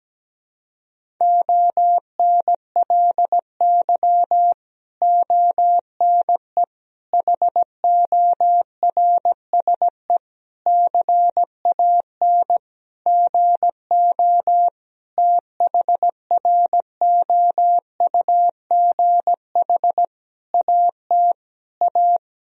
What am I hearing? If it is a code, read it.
Morse code, heard as a keyed tone, about 17 wpm, 700 Hz. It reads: ONLY ONE HORSE CAN GO THROUGH AT A
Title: Sentences from Top 1000 Words - Review Part 1 - 17wpm